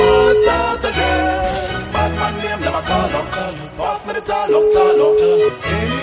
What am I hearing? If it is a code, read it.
Ringback tone (Europe, 425 Hz), with noise.